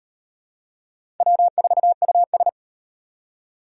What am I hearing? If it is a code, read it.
Morse code, heard as a keyed tone, about 38 wpm, 700 Hz. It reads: W4US